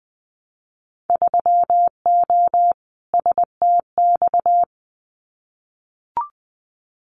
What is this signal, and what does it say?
Morse code, heard as a keyed tone, about 20 wpm, 700 Hz. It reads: 3O STX
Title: Winter Field Day Exchanges 20wpm